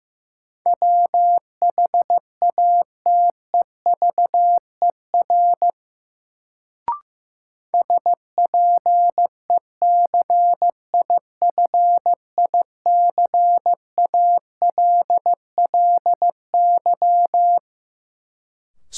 Morse code, heard as a keyed tone, about 15 wpm, 700 Hz. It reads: WHATEVER SPECIFICALLY
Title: Sentences from Top 1100 Words - New Word Drill Alt - 15wpm